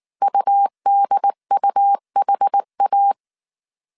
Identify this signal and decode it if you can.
Morse code, heard as a keyed tone, about 19 wpm, 800 Hz. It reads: UDUHA